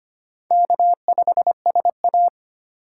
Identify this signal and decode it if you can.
Morse code, heard as a keyed tone, about 32 wpm, 700 Hz. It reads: K5SA